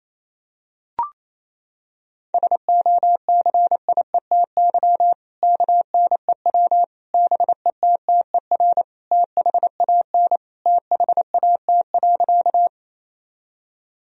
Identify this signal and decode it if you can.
Morse code, heard as a keyed tone, about 28 wpm, 700 Hz. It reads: SOCIETY KNEW BETTER THAN THAT.
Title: Sentences from Top 1000 Words - Review Part 4 - 28wpm